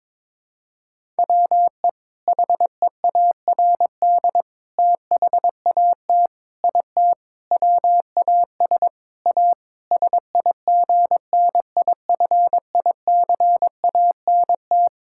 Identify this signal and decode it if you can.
Morse code, heard as a keyed tone, about 22 wpm, 700 Hz. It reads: WE HEARD THAT IT WAS A SIGNIFICANT
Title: Sentences from Top 1000 Words - Review Part 6 - 22wpm